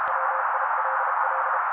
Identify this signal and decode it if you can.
Morse code, heard as a keyed tone, about 32 wpm, 530 Hz. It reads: WILL
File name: daily / 31wpm noise